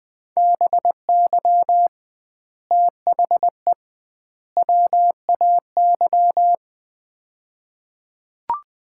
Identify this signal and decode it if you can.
Morse code, heard as a keyed tone, about 20 wpm, 700 Hz. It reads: BY THE WAY
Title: 2x Word-Spacing - Sets of 3 Words - 20wpm